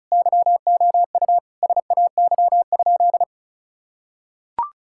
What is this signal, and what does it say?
Morse code, heard as a keyed tone, about 36 wpm, 700 Hz. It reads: YOU SAY?
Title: Sentences from Top 1600 Words 35wpm